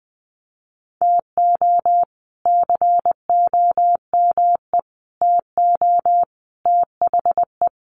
Morse code, heard as a keyed tone, about 20 wpm, 700 Hz. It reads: TO COME TO THE
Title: Sentences from Top 900 Words 20wpm